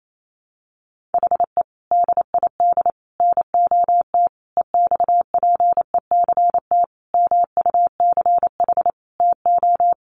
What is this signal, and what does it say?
Morse code, heard as a keyed tone, about 28 wpm, 700 Hz. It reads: HE DID NOT EXPECT MUCH TO
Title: Sentences from Top 1100 Words 28wpm